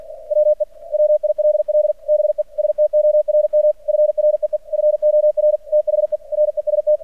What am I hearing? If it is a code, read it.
Morse code, heard as a keyed tone, about 45 wpm, 600 Hz. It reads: WE WILL BE STORM FREE FOR THE REST
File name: archived / CRSnet